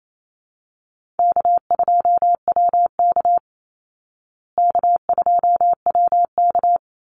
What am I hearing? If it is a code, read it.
Morse code, heard as a keyed tone, about 28 wpm, 700 Hz. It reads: K2WK K2WK
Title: Easy Call Signs Repeated 3x - 28wpm